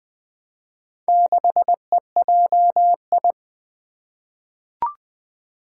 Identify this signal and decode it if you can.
Morse code, heard as a keyed tone, about 20 wpm, 700 Hz. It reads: 6EJI
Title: Four Challenging Letter-Number Combinations 20wpm